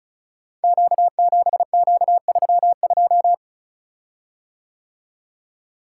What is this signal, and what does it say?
Morse code, heard as a keyed tone, about 36 wpm, 700 Hz. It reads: Q7Q32
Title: Code Groups - 5 Letter:Numbers - 35wpm